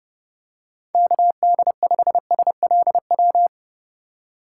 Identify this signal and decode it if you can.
Morse code, heard as a keyed tone, about 30 wpm, 700 Hz. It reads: KD5SLW
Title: USA Call Signs - 3x Character Spacing - 30wpm